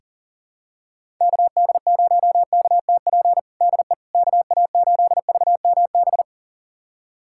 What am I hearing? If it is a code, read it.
Morse code, heard as a keyed tone, about 37 wpm, 700 Hz. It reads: KD0KTP DE KA8VMB
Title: USA Callsign DE Callsign 40wpm